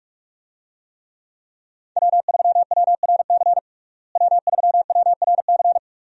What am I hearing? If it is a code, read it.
Morse code, heard as a keyed tone, about 46 wpm, 700 Hz. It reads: W3WRC W3WRC
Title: USA Call Signs - 5 Characters - Repeated 2x - 45wpm